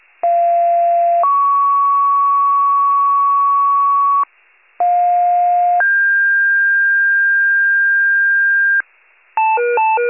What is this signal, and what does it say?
Two-tone page, heard as a one-second tone, then a three-second tone, about 690 then 1120 Hz, about 710 then 1600 Hz, with noise. It is heard from 0.2 to 8.8 s.
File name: 46 RESCUE DUTY